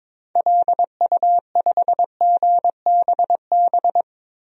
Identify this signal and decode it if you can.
Morse code, heard as a keyed tone, about 22 wpm, 700 Hz. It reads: LU5GBB
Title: DXCC-100 Call Signs - 6 Characters - 22wpm